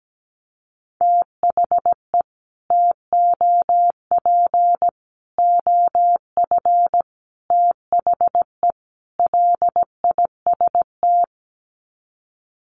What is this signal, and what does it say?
Morse code, heard as a keyed tone, about 17 wpm, 700 Hz. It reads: THE TOP OF THE LIST
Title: Sets of 5 Words Encore2 17wpm